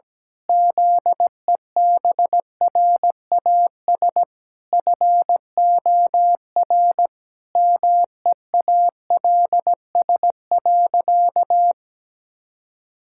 Morse code, heard as a keyed tone, about 17 wpm, 700 Hz. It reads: ZEBRAS FOR MEALS.
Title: Pangram Challenge v2 - 17wpm